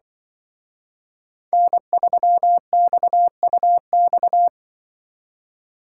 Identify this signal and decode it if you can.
Morse code, heard as a keyed tone, about 24 wpm, 700 Hz. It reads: N3XUX
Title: Speed Racing - USA 1x3 Call Signs 20wpm